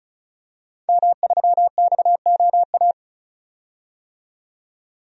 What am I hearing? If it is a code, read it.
Morse code, heard as a keyed tone, about 34 wpm, 700 Hz. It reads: M3XOA
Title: DXCC-100 Call Signs - 5 Characters - 35wpm